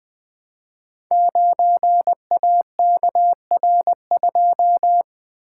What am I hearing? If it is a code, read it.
Morse code, heard as a keyed tone, about 20 wpm, 700 Hz. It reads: 9AKR2
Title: Code Groups - 5 Letter:Numbers - 20wpm